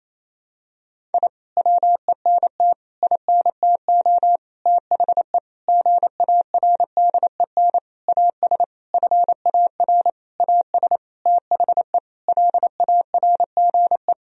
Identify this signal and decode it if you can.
Morse code, heard as a keyed tone, about 27 wpm, 700 Hz. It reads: I WENT INTO THE GARDEN AS FAR AS THE LARGE
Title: Sentences from Top 1000 Words - Review Part 4 - 28wpm